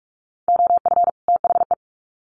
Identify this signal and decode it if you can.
Morse code, heard as a keyed tone, about 44 wpm, 700 Hz. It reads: OF THE
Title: Sets of 2 Words 45wpm